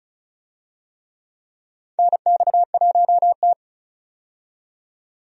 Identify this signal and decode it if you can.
Morse code, heard as a keyed tone, about 36 wpm, 700 Hz. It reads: NX1T
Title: USA Call Signs - 4 Characters - 35wpm